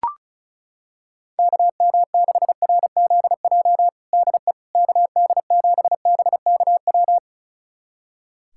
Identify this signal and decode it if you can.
Morse code, heard as a keyed tone, about 34 wpm, 700 Hz. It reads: KM6RZJ DE KD7BKW
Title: USA Callsign DE Callsign - 6 Characters - 35wpm